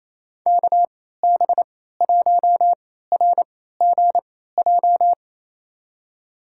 Morse code, heard as a keyed tone, about 28 wpm, spaced as for 16 wpm, 700 Hz. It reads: KB1RGJ
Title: USA Call Signs - 6 Characters - 3x Character Spacing - 28wpm